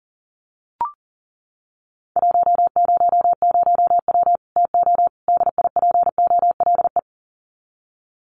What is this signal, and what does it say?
Morse code, heard as a keyed tone, about 40 wpm, 700 Hz. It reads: 100W TO DIPOLE
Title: Common QSO Phrases 40wpm